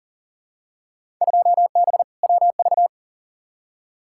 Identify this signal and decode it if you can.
Morse code, heard as a keyed tone, about 40 wpm, 700 Hz. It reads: 2B WV
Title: Field Day Exchanges Encore - 40wpm